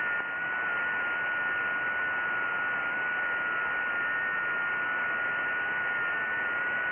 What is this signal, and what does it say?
FT8.
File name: IC-7100